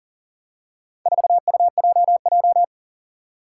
Morse code, heard as a keyed tone, about 41 wpm, 700 Hz. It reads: 4UJJ
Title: Four Challenging Letter-Number Combinations 40wpm